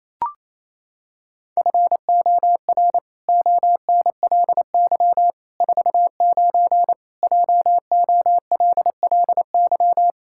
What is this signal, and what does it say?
Morse code, heard as a keyed tone, about 28 wpm, 700 Hz. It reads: FOR ONLY 49 JOLLY
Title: Pangram Challenge v2 - 28wpm